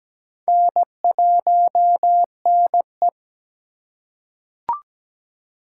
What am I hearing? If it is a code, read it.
Morse code, heard as a keyed tone, about 17 wpm, 700 Hz. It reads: N1NE
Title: USA Call Signs - 4 Characters - 2x Character Spacing - 17wpm